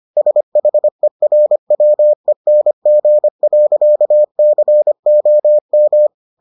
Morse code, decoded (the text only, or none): SHERWENG.COM